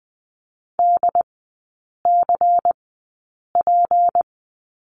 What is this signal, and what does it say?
Morse code, heard as a keyed tone, about 20 wpm, spaced as for 8 wpm, 700 Hz. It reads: DCP
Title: 152 Lesson 25e - Letter p - Three Characters 20wpm